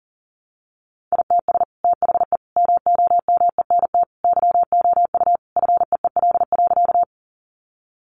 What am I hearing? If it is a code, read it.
Morse code, heard as a keyed tone, about 40 wpm, 700 Hz. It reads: ITS THE MOMENT YOU FEEL.
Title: Sentences from Top 1000 Words 40wpm